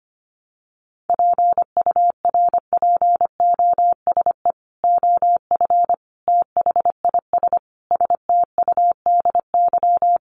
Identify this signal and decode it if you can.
Morse code, heard as a keyed tone, about 25 wpm, 700 Hz. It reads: PURPOSE OF THIS STUDY
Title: Sets of 5 Words 25wpm